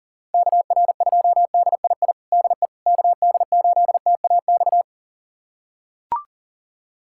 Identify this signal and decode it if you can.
Morse code, heard as a keyed tone, about 37 wpm, 700 Hz. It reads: KR2DII DE KD8TAX
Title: USA Callsign DE Callsign 40wpm